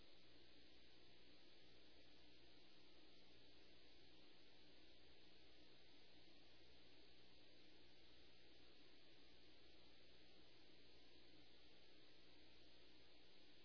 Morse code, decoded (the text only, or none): none